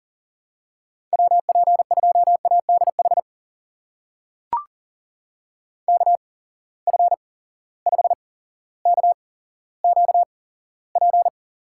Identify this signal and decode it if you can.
Morse code, heard as a keyed tone, about 40 wpm, spaced as for 17 wpm, 700 Hz. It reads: WP2ADH KF5KQP
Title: USA Call Signs - 6 Characters - 8x Character Spacing - 40wpm